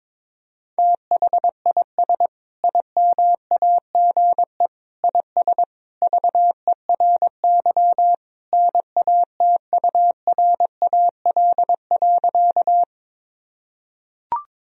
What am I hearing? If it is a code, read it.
Morse code, heard as a keyed tone, about 22 wpm, 700 Hz. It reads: THIS IMAGE IS VERY NATURAL.
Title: Sentences from Top 1000 Words - Review Part 4 - 22wpm